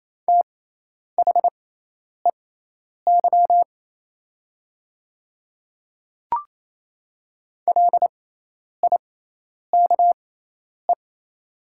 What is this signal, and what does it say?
Morse code, heard as a keyed tone, about 28 wpm, spaced as for 10 wpm, 700 Hz. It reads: THEY LIKE
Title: 6x Character Spacing - Top 100 Words - 28wpm